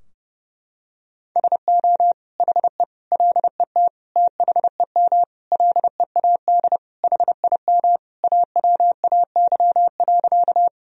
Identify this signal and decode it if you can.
Morse code, heard as a keyed tone, about 30 wpm, 700 Hz. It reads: SO HE LET THEM LEAD HIM AWAY.
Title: Sentences from Top 1000 Words - Review Part 2 - 30wpm